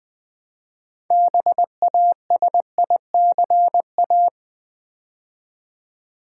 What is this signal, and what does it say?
Morse code, heard as a keyed tone, about 20 wpm, 700 Hz. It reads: BASICA
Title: Build A Word - 3 Letters At A Time - 9-Letter Words - 20wpm